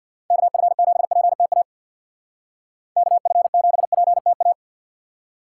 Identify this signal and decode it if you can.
Morse code, heard as a keyed tone, about 56 wpm, 700 Hz. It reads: KF7PTA KF7PTA
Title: Speed Racing - USA 2x3 Call Signs 50wpm